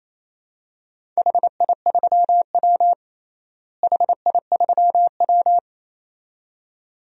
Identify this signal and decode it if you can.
Morse code, heard as a keyed tone, about 28 wpm, 700 Hz. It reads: HI3W HI3W
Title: DXCC-100 Call Signs - 4 Characters - Repeated 2x - 28wpm